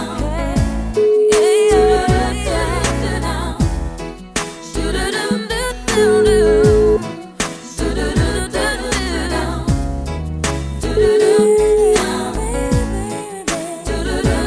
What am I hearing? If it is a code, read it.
Ringback tone (Europe, 425 Hz), with noise.